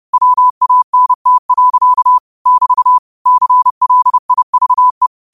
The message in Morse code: WANT. <BT> CLIVE

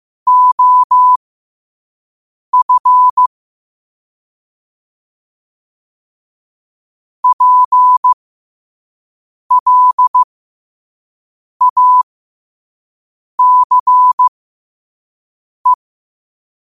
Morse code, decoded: OF PLACE

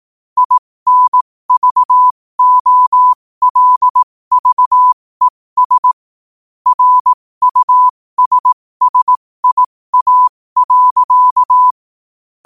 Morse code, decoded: INVOLVES RUSSIA.

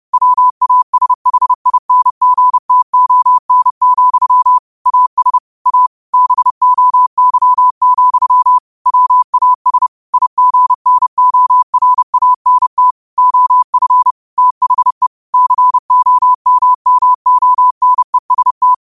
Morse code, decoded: WASHINGTON, AS A BOY, WAS IGNORANT OF THE COMMONEST

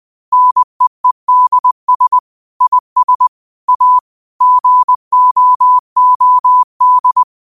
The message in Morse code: NEEDS IS A GOOD

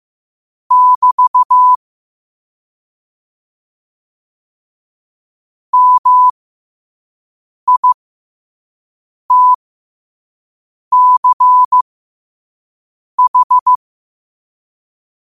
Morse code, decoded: <BT> MITCH

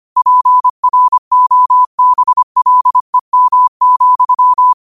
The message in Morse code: PROBLEM,

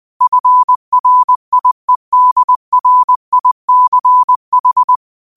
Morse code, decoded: FRIEDRICH

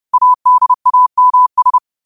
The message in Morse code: ADAMS

